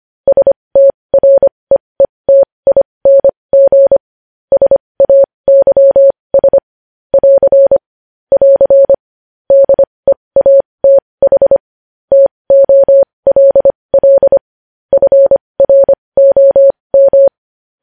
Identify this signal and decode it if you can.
Morse code, heard as a keyed tone, about 25 wpm, spaced as for 20 wpm, 550 Hz. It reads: STREETING SAYS <AR> <AR> DEATH TOLL FROM
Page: News Headlines in Morse Code at 20 WPM